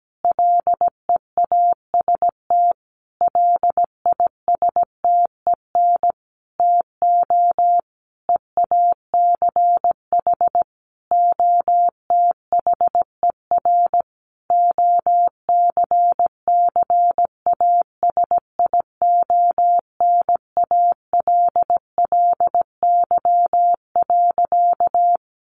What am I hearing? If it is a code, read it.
Morse code, heard as a keyed tone, about 17 wpm, 700 Hz. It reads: LEAST LISTEN TO EACH OTHER OCCASIONALLY.
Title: Sentences from Top 1300 Words 17wpm